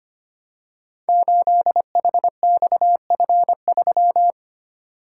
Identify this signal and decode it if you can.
Morse code, heard as a keyed tone, about 25 wpm, 700 Hz. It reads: 8HXF3
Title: Code Groups - 5 Letter:Numbers - 25wpm